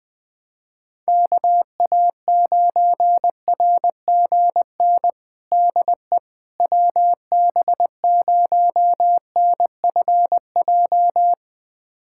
Morse code, heard as a keyed tone, about 21 wpm, 700 Hz. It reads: KA9RGN DE WB0NFJ
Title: USA Callsign DE Callsign - 6 Characters - 20wpm